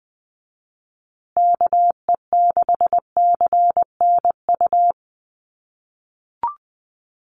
Morse code, heard as a keyed tone, about 20 wpm, 700 Hz. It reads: KE6CNU
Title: USA Call Signs Repeated 2x 20wpm